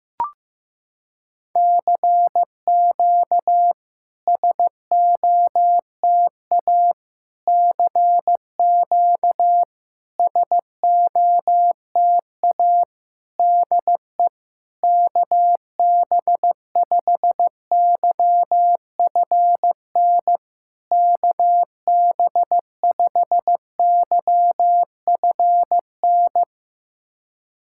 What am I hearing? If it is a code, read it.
Morse code, heard as a keyed tone, about 15 wpm, 700 Hz. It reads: CQ SOTA CQ SOTA DE KB5YFN KB5YFN
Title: SOTA QSO Practice 15wpm